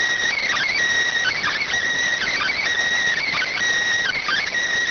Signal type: Emergency Alert System (SAME) tones

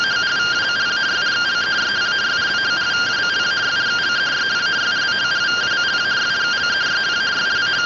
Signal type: two-tone teleprinter signal

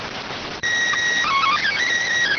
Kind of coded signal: SSTV (slow-scan television)

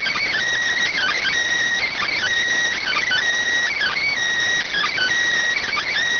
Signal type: Emergency Alert System (SAME) tones

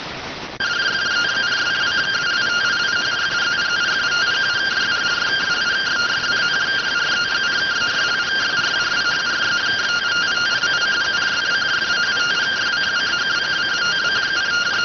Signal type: two-tone teleprinter signal

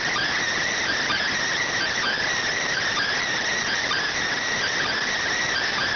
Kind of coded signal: Emergency Alert System (SAME) tones

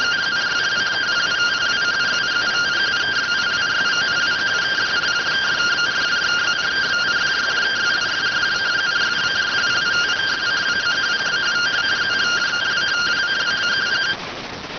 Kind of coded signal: two-tone teleprinter signal

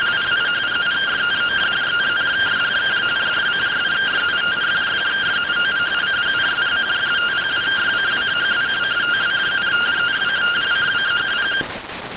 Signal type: two-tone teleprinter signal

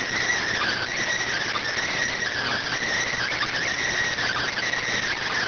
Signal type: Emergency Alert System (SAME) tones

